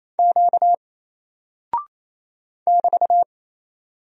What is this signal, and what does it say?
Morse code, heard as a keyed tone, about 27 wpm, 700 Hz. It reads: Q <BT>
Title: Single Character (Letter:Number:Punctuation:Prosign) 28wpm